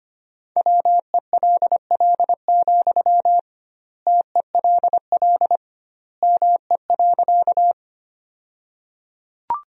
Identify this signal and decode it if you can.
Morse code, heard as a keyed tone, about 25 wpm, 700 Hz. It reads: WELL, TELL ME.
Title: 2x Word-Spacing - Sentences from Top 200 Words - 25wpm